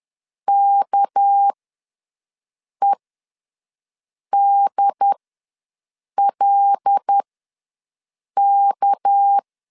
Morse code, heard as a keyed tone, about 10 wpm, spaced as for 5 wpm, 800 Hz. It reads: KEDLK
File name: LEZIONI / 6 PKX0 / 6 PK ws 10wpm 3745